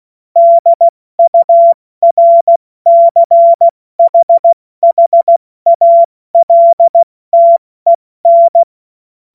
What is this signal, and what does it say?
Morse code, heard as a keyed tone, about 16 wpm, 680 Hz. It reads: DURCHHALTEN
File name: Morsekurs / Witze / Witz 02 April 2024 70 F80